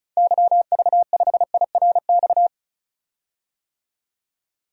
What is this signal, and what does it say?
Morse code, heard as a keyed tone, about 34 wpm, 700 Hz. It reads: YV5IRX
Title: DXCC-100 Call Signs - 6 Characters - 35wpm